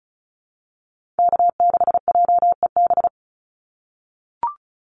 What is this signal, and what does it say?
Morse code, heard as a keyed tone, about 34 wpm, 700 Hz. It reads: K6JEB
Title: Call Signs - 35wpm